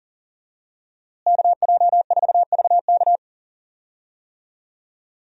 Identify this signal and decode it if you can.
Morse code, heard as a keyed tone, about 40 wpm, 700 Hz. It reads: KJ4VK
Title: USA Call Signs - 5 Characters - Repeated 3x - 40wpm